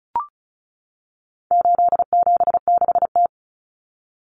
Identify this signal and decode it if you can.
Morse code, heard as a keyed tone, about 34 wpm, 700 Hz. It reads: 876T
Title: Common Cut Numbers - 4-Digit 35wpm